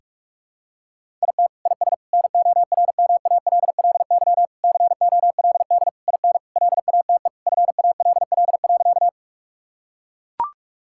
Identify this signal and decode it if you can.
Morse code, heard as a keyed tone, about 41 wpm, 700 Hz. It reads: IT IS NORMALLY COLD IN LATE FALL.
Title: Sentences from Top 1000 Words - Review Part 6 - 45wpm